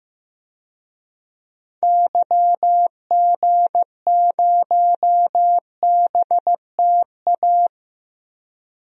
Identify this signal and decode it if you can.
Morse code, heard as a keyed tone, about 15 wpm, 700 Hz. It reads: YG0BTA
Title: DXCC-100 Call Signs - 6 Characters - 15wpm